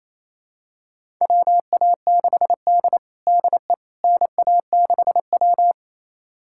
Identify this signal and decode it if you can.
Morse code, heard as a keyed tone, about 28 wpm, 700 Hz. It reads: WA6D DE NA6W
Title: USA Callsign DE Callsign - 4 Characters - 28wpm